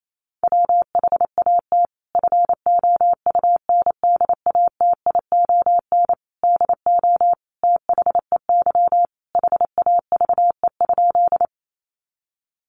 Morse code, heard as a keyed tone, about 28 wpm, 700 Hz. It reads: WHAT FOUNDATION DO THEY HAVE?
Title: Sentences from Top 1200 Words 28wpm